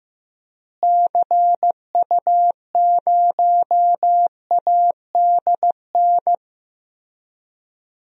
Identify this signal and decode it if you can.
Morse code, heard as a keyed tone, about 15 wpm, 700 Hz. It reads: CU0ADN
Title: DXCC-100 Call Signs - 6 Characters - Repeated 3x - 15wpm